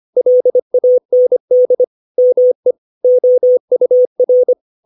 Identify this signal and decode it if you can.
Morse code, heard as a keyed tone, about 25 wpm, 500 Hz. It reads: LAND ME OUR
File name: 632 words / 632 Words 25WPM